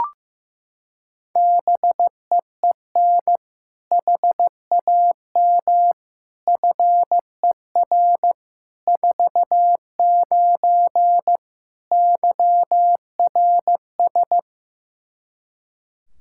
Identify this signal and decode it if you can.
Morse code, heard as a keyed tone, about 15 wpm, 700 Hz. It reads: BEEN HAM FER 49 YRS
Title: K1USN SST QSO Practice 15wpm